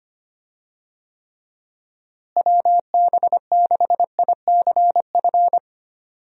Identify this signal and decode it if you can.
Morse code, heard as a keyed tone, about 25 wpm, 700 Hz. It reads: WB6ICF
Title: USA Call Signs - 6 Characters - Repeated 2x - 25wpm